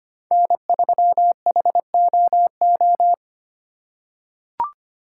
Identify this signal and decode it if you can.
Morse code, heard as a keyed tone, about 25 wpm, 700 Hz. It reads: N3HOO
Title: USA Call Signs - 5 Characters - 3x Character Spacing - 25wpm